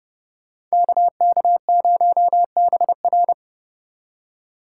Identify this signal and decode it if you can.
Morse code, heard as a keyed tone, about 37 wpm, 700 Hz. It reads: KK0BR
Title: Speed Racing - USA 2x2 Call Signs 30wpm